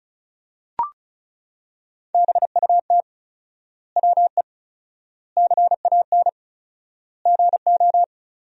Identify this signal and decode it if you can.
Morse code, heard as a keyed tone, about 36 wpm, 700 Hz. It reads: BUT WE CAN GO
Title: 4x Word-Spacing - Sentences from Top 100 Words - 35wpm